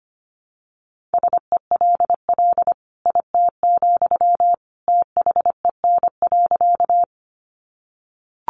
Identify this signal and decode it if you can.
Morse code, heard as a keyed tone, about 25 wpm, 700 Hz. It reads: SELL IT, THEN.
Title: Sentences from Top 1100 Words 25wpm